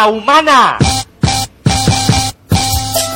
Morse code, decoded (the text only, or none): none